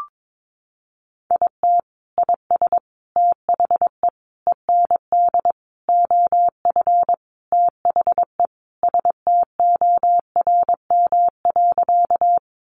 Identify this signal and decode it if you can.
Morse code, heard as a keyed tone, about 22 wpm, 700 Hz. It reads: IT IS THE END OF THE STORM.
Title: Sentences from Top 1500 Words 22wpm